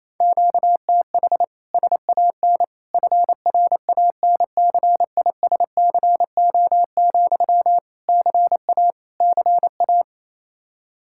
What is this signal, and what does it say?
Morse code, heard as a keyed tone, about 27 wpm, 700 Hz. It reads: QTH SAN FRANCISCO, CA CA
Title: K1USN SST QSO Practice 28wpm